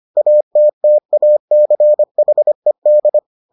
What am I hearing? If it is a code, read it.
Morse code, heard as a keyed tone, about 25 wpm, 600 Hz. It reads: ATTACHED